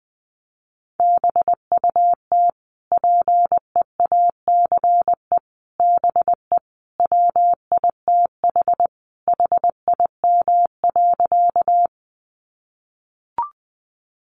Morse code, heard as a keyed tone, about 20 wpm, 700 Hz. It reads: BUT PEACE BE WITH HIM.
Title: Sentences from Top 1500 Words 20wpm